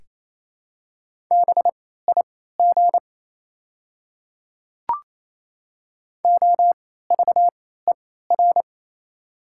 Morse code, heard as a keyed tone, about 27 wpm, spaced as for 16 wpm, 700 Hz. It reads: BIG OVER
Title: 3x Character Spacing - Top 200 Words - 28wpm